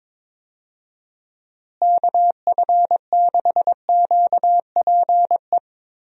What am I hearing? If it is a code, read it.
Morse code, heard as a keyed tone, about 22 wpm, 700 Hz. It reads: KF6QPE
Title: USA Call Signs - 6 Characters - Repeated 3x - 22wpm